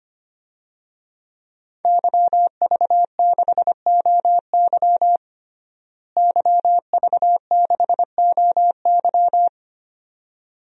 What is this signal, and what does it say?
Morse code, heard as a keyed tone, about 25 wpm, 700 Hz. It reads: YV6OY YV6OY
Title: DXCC-100 Call Signs - 5 Characters - Repeated 2x - 25wpm